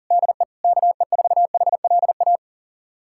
Morse code, heard as a keyed tone, about 37 wpm, 700 Hz. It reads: DE KE4HLA
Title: USA Callsign DE Callsign - 6 Characters - 40wpm